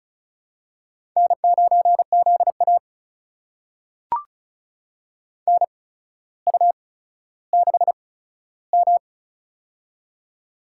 Morse code, heard as a keyed tone, about 36 wpm, spaced as for 15 wpm, 700 Hz. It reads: N9ZA NU6M
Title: USA Call Signs - 4 Characters - 8x Character Spacing - 35wpm